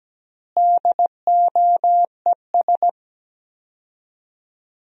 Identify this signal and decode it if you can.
Morse code, heard as a keyed tone, about 17 wpm, 700 Hz. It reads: DOES
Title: 4x Word-Spacing - Sentences from Top 300 Words - 17wpm